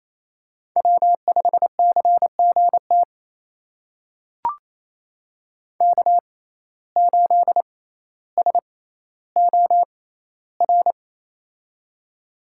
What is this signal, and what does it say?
Morse code, heard as a keyed tone, about 28 wpm, spaced as for 14 wpm, 700 Hz. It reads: W5CGT K8SOR
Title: USA Call Signs - 5 Characters - 6x Character Spacing - 28wpm